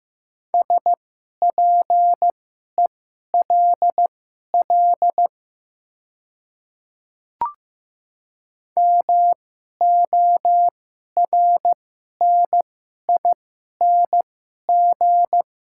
Morse code, heard as a keyed tone, about 15 wpm, spaced as for 11 wpm, 700 Hz. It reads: SPELL MORNING
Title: 2x Character Spacing - Top 500 Words - 15wpm